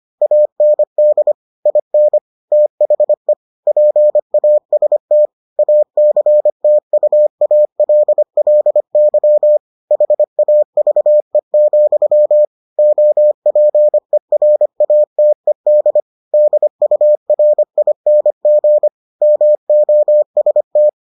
Morse code, heard as a keyed tone, about 25 wpm, 600 Hz. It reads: AND IN THE PAST ACTUALLY HAVE, OPERATED DURING MOST